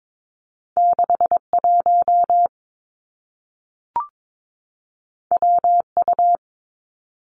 Morse code, heard as a keyed tone, about 22 wpm, 700 Hz. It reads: 6 1 W U
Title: Two Challenging Letter-Number Combinations 22wpm